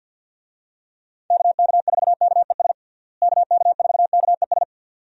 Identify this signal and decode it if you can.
Morse code, heard as a keyed tone, about 50 wpm, 700 Hz. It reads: KK4KES KK4KES
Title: USA Call Signs - 6 Characters - Repeated 2x - 50wpm